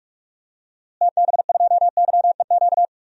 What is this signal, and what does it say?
Morse code, heard as a keyed tone, about 45 wpm, 700 Hz. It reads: TB2YEQ
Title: DXCC-100 Call Signs - 6 Characters - Repeated 2x - 45wpm